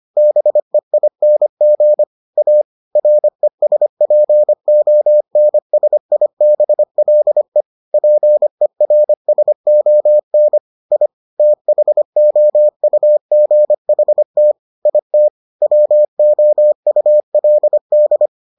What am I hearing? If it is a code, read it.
Morse code, heard as a keyed tone, about 25 wpm, 600 Hz. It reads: BEING A RESPONSIBLE PERSON I THOUGHT IT WOULD